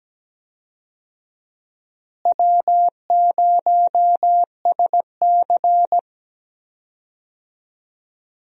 Morse code, heard as a keyed tone, about 17 wpm, 700 Hz. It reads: W0SC
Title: USA Call Signs - 4 Characters - Repeated 3x - 17wpm